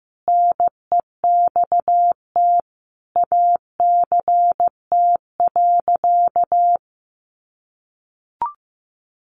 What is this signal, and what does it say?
Morse code, heard as a keyed tone, about 15 wpm, 700 Hz. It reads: NEXT ACT.
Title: Sentences from Top 900 Words 15wpm